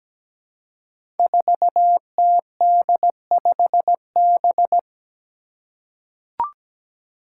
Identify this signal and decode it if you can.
Morse code, heard as a keyed tone, about 17 wpm, 700 Hz. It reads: 4TD5B
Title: Five Challenging Letter-Number Combinations 17wpm